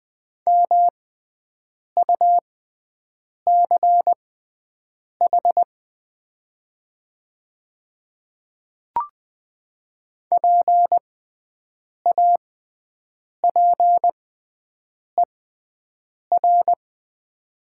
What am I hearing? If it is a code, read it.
Morse code, heard as a keyed tone, about 20 wpm, spaced as for 7 wpm, 700 Hz. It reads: MUCH PAPER
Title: 6x Character Spacing - Top 500 Words - 20wpm